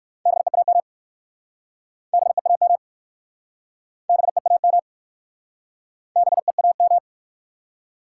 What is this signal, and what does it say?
Morse code, heard as a keyed tone, about 59 wpm, 700 Hz. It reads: BEAM BEAM BEAM BEAM